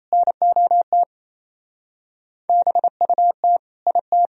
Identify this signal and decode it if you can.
Morse code, heard as a keyed tone, about 27 wpm, 700 Hz. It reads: NOT BUT IT
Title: Speed-Racing - Sets of 4 Words 28wpm